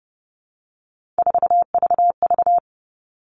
Morse code, heard as a keyed tone, about 30 wpm, 700 Hz. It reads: 4VV
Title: All Cut Numbers - 3-Digit 30wpm